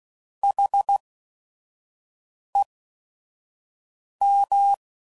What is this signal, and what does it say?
Morse code, heard as a keyed tone, about 16 wpm, spaced as for 5 wpm, 800 Hz. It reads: HEM